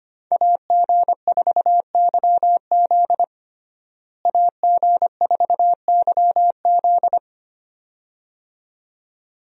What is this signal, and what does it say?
Morse code, heard as a keyed tone, about 25 wpm, 700 Hz. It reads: AG4YZ AG4YZ
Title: USA Call Signs - 5 Characters - Repeated 2x - 25wpm